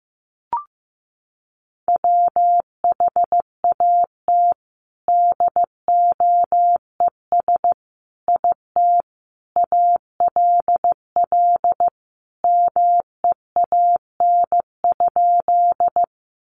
Morse code, heard as a keyed tone, about 15 wpm, 700 Hz. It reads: WHAT DOES IT ALL MEAN?
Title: Sentences from Top 200 Words 15wpm